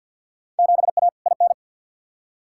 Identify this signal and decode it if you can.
Morse code, heard as a keyed tone, about 52 wpm, 700 Hz. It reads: 7A IN
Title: Field Day Exchanges Encore - 50wpm